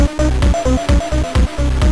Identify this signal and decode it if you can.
There is no code.